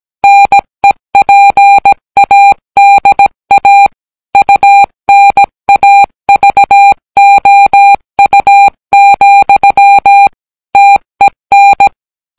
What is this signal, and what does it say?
Morse code, heard as a keyed tone, about 18 wpm, 800 Hz. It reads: NEPADA UNAVOU, TEN